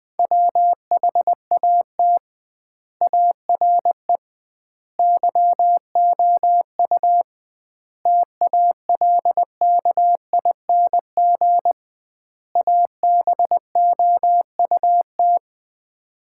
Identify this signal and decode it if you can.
Morse code, heard as a keyed tone, about 21 wpm, 700 Hz. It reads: WHAT ARE YOU TALKING ABOUT
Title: 2x Word-Spacing - Sets of 5 Words - 20wpm